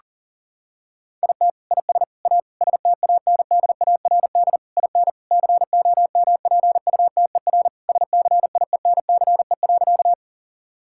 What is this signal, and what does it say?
Morse code, heard as a keyed tone, about 40 wpm, 700 Hz. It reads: IT IS A STANDARD IN COMPUTER SCIENCE.
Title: Sentences from Top 1000 Words - Review Part 6 - 40wpm